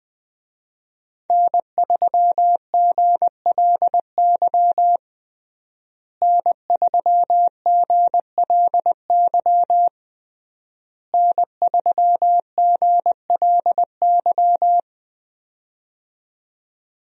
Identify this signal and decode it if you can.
Morse code, heard as a keyed tone, about 20 wpm, 700 Hz. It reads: N3GLY N3GLY N3GLY
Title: USA Call Signs - 5 Characters - Repeated 3x - 20wpm